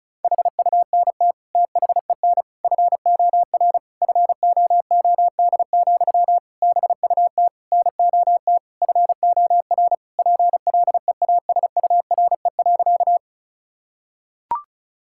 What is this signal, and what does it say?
Morse code, heard as a keyed tone, about 34 wpm, 700 Hz. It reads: HUNT THEN FOR FOOD, BUT NOT FOR PLEASURE.
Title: Sentences from Top 1600 Words 35wpm